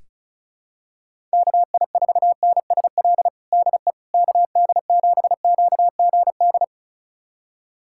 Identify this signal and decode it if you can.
Morse code, heard as a keyed tone, about 36 wpm, 700 Hz. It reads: KI4NSL DE KD7QGD
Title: USA Callsign DE Callsign - 6 Characters - 35wpm